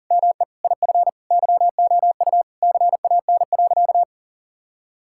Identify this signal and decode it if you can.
Morse code, heard as a keyed tone, about 40 wpm, 700 Hz. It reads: ME IF YOU CAN.
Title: Sentences from Top 1600 Words 40wpm